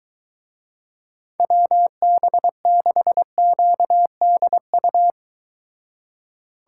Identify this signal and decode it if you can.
Morse code, heard as a keyed tone, about 23 wpm, 700 Hz. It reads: WB6QDU